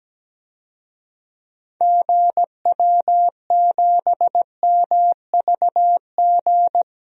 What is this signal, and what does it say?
Morse code, heard as a keyed tone, about 17 wpm, 700 Hz. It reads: GW7MVG
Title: DXCC-100 Call Signs - 6 Characters - 17wpm